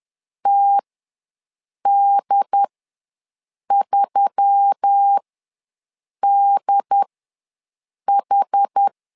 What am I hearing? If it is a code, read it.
Morse code, heard as a keyed tone, about 10 wpm, spaced as for 6 wpm, 800 Hz. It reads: TD3DH
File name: LEZIONI / 4 Q1S3 ws 10wpm 2437